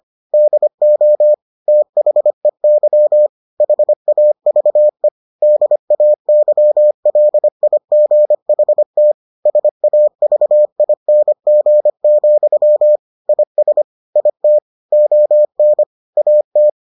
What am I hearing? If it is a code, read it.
Morse code, heard as a keyed tone, about 25 wpm, 600 Hz. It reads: DO THEY HAVE DAYLIGHT SAVING, IS IT ON AT